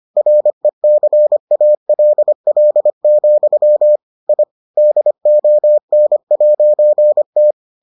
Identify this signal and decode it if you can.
Morse code, heard as a keyed tone, about 25 wpm, 600 Hz. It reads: RECALL, I DON'T